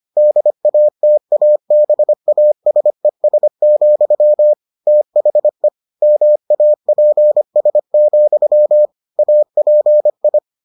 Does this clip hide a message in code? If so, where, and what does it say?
Yes: Morse code from 0.2 to 10.4 s, heard as a keyed tone, about 25 wpm, 600 Hz. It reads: DATABASES, THE MAPS, API